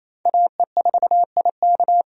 Morse code, heard as a keyed tone, about 28 wpm, 700 Hz. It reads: AE4IK
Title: USA Call Signs - 5 Characters - 28wpm